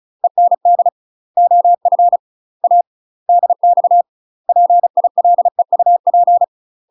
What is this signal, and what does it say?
Morse code, heard as a keyed tone, about 36 wpm, 700 Hz. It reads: END OF A DX PILEUP
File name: UK MASTER QSOs TRAINING / 35WPM / UK MASTER QSOs 0016